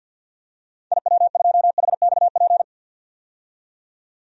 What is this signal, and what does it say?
Morse code, heard as a keyed tone, about 48 wpm, 700 Hz. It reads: IW2HXP
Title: DXCC-100 Call Signs - 6 Characters - 50wpm